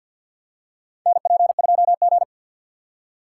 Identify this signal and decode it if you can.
Morse code, heard as a keyed tone, about 52 wpm, 700 Hz. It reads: NP2G
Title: USA Call Signs - 4 Characters - 3x Character Spacing - 50wpm